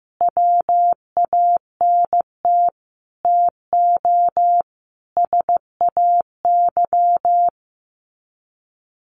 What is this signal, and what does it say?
Morse code, heard as a keyed tone, about 15 wpm, 700 Hz. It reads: WANT TO SAY
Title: Sets of 5 Words Encore 15wpm